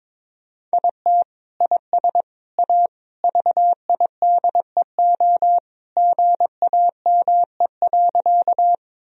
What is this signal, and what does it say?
Morse code, heard as a keyed tone, about 22 wpm, 700 Hz. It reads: IT IS A VIDEO GAME.
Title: Sentences from Top 1000 Words - Review Part 4 - 22wpm